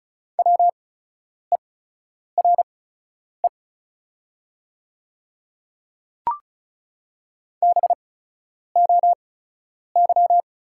Morse code, heard as a keyed tone, about 36 wpm, spaced as for 10 wpm, 700 Hz. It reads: WERE BOY
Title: 8x Character Spacing - Top 200 Words - 35wpm